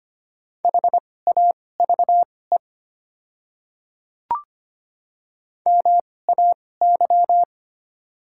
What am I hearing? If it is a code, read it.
Morse code, heard as a keyed tone, about 25 wpm, spaced as for 18 wpm, 690 Hz. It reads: HAVE MAY
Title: 2x Character Spacing - Top 100 Words - 25wpm